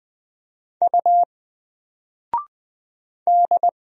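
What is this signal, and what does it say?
Morse code, heard as a keyed tone, about 20 wpm, 700 Hz. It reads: U D
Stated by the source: Single letter and number Morse drill at 20 wpm, from N4LCD.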